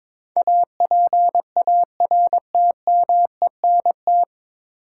Morse code, heard as a keyed tone, about 22 wpm, 700 Hz. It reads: APARTMENT
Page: Morse Code Ninja practice 22wpm